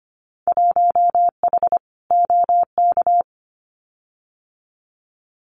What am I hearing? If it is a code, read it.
Morse code, heard as a keyed tone, about 25 wpm, 700 Hz. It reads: 1H OK
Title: Winter Field Day Exchanges 25wpm